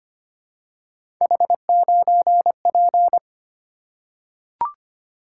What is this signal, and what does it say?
Morse code, heard as a keyed tone, about 25 wpm, 700 Hz. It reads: H9P
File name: Three Letter-Number Combinations - 25wpm